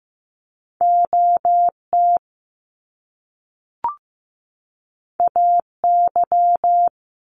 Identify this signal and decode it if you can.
Morse code, heard as a keyed tone, about 15 wpm, 700 Hz. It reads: OT AY
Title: Top Two Letter Combinations in Words 15wpm